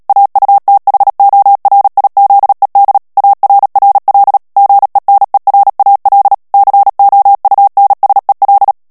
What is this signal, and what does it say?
Morse code, heard as a keyed tone, about 37 wpm, 800 Hz. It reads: AUTHORIZED ARRL GENERAL COUNSEL
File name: morse mp3 / ANGLAIS texte clair 3 vitesse 35 mots minute